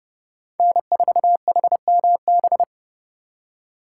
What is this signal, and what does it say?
Morse code, heard as a keyed tone, about 36 wpm, 700 Hz. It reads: N4HMB
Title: Speed Racing - USA 1x3 Call Signs 30wpm